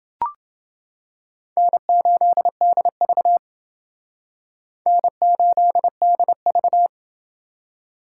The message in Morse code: N8DV N8DV

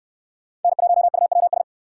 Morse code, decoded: N1UWS

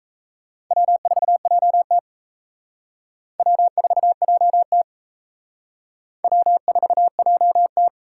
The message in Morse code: W4JT W4JT W4JT

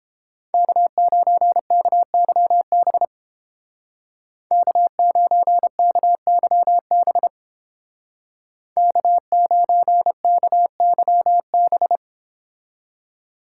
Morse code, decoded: K9KYB K9KYB K9KYB

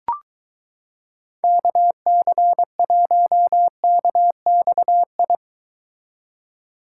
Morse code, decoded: KC1KXI

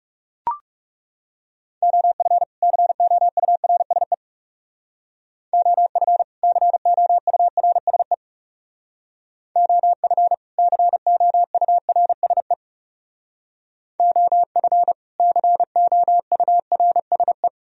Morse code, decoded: OF COURSE OF COURSE OF COURSE OF COURSE

